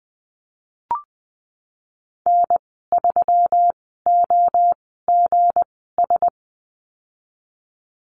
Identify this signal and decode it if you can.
Morse code, heard as a keyed tone, about 20 wpm, spaced as for 15 wpm, 700 Hz. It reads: N3OGS
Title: 167 Lesson 27g - Number 7 - Call Signs 20wpm